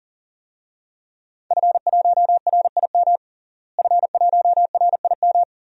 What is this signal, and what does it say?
Morse code, heard as a keyed tone, about 37 wpm, 700 Hz. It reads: F1RIM F1RIM
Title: DXCC-100 Call Signs - 5 Characters - Repeated 2x - 40wpm